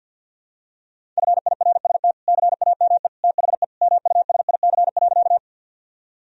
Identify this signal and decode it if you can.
Morse code, heard as a keyed tone, about 48 wpm, 700 Hz. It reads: FIRST CAME THE MUSIC.
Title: Sentences from Top 1000 Words - Review Part 1 - 50wpm